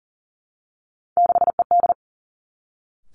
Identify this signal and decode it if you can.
Morse code, heard as a keyed tone, about 38 wpm, 700 Hz. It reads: 6ED
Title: Three Challenging Letter-Number Combinations 40wpm